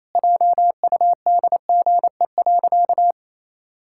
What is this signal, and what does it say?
Morse code, heard as a keyed tone, about 28 wpm, 700 Hz. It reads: JUDGE.
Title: Pangram Challenge v2 - 28wpm